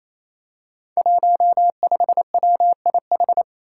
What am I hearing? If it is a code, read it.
Morse code, heard as a keyed tone, about 27 wpm, 700 Hz. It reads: 15WIH
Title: Five Challenging Letter-Number Combinations 28wpm